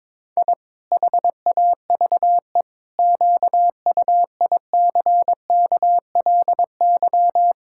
Morse code, heard as a keyed tone, about 22 wpm, 700 Hz. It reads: I HAVE QUICKLY
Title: Pangram Challenge v2 - 22wpm